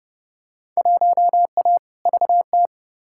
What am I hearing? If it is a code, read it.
Morse code, heard as a keyed tone, about 30 wpm, 700 Hz. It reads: 1A VT
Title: Field Day Exchanges Encore - 30wpm